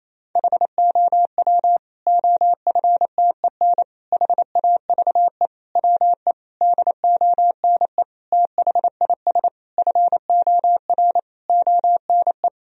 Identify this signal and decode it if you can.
Morse code, heard as a keyed tone, about 28 wpm, 700 Hz. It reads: HOW OFTEN HAVE WE DONE THIS FOR ONE
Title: Sentences from Top 1000 Words - Review Part 1 - 28wpm